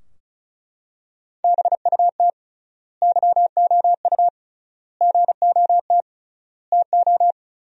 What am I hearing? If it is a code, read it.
Morse code, heard as a keyed tone, about 36 wpm, 700 Hz. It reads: BUT YOU GOT TO
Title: 3x Word-Spacing - Sentences from Top 300 Words - 35wpm